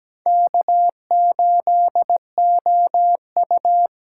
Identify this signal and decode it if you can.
Morse code, heard as a keyed tone, about 17 wpm, 700 Hz. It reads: K8OU
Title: USA Call Signs - 4 Characters - 4x Character Spacing - 17wpm